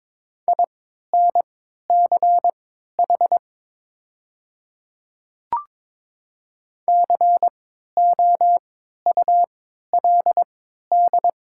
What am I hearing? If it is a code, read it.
Morse code, heard as a keyed tone, about 22 wpm, spaced as for 12 wpm, 700 Hz. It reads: INCH COULD
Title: 3x Character Spacing - Top 500 Words - 22wpm